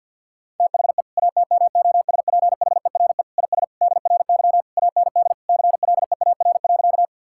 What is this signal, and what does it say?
Morse code, heard as a keyed tone, about 52 wpm, 700 Hz. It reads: THE ATMOSPHERE IS DRY AND CLEAR.
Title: Sentences from Top 1600 Words 50wpm